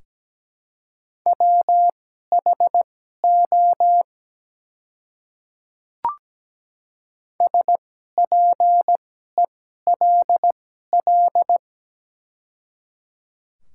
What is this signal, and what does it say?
Morse code, heard as a keyed tone, about 17 wpm, spaced as for 12 wpm, 700 Hz. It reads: WHO SPELL
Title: 2x Character Spacing - Top 200 Words - 17wpm